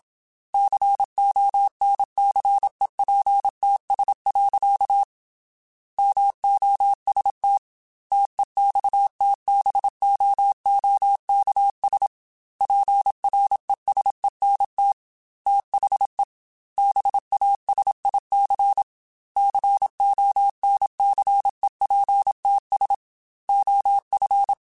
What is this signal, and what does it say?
Morse code, heard as a keyed tone, about 27 wpm, 800 Hz. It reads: CONCEPTS. MOST TEXTBOOKS PRESENT THE BASIC CONCEPTS OF
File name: Technical Subjects / Technical Subjects-mp3s / Gates Intro @25